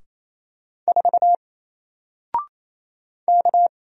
Morse code, heard as a keyed tone, about 27 wpm, 700 Hz. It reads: 4 K
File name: Single-Letter-Number-28wpm